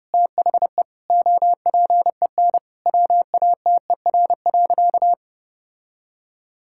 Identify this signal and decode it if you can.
Morse code, heard as a keyed tone, about 30 wpm, 700 Hz. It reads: THE OPEN WATER.
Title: Sentences from Top 1000 Words - Review Part 6 - 30wpm